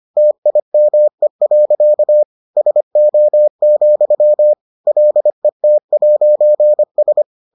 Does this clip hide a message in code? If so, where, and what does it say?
Yes: Morse code from 0.2 to 7.2 s, heard as a keyed tone, about 25 wpm, 600 Hz. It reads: TIME. SO, LET'S